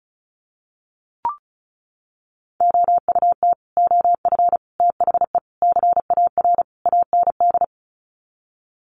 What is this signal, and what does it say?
Morse code, heard as a keyed tone, about 34 wpm, 700 Hz. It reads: OUT OF THE CAR AND
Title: Sets of 5 Words Encore 35wpm